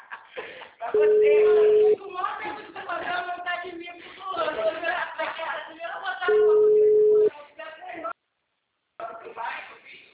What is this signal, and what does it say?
Ringback tone (Europe, 425 Hz), with speech.